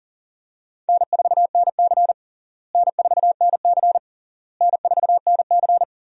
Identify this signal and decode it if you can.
Morse code, heard as a keyed tone, about 40 wpm, 700 Hz. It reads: N4NC N4NC N4NC
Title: USA Call Signs - 4 Characters - Repeated 3x - 40wpm